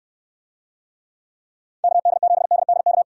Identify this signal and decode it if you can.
Morse code, heard as a keyed tone, about 69 wpm, 700 Hz. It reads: KD7DDK